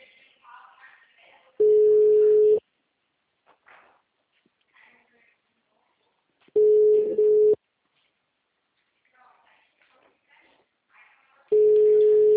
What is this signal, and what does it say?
Ringback tone (Europe, 425 Hz), with speech.